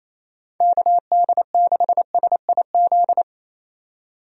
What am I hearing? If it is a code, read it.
Morse code, heard as a keyed tone, about 28 wpm, 700 Hz. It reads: KD6SIZ
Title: USA Call Signs - 6 Characters - 28wpm